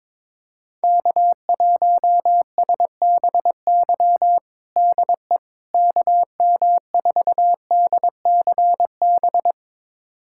Morse code, heard as a keyed tone, about 22 wpm, 700 Hz. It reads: K1SBY DE KM4DCB
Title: USA Callsign DE Callsign 22wpm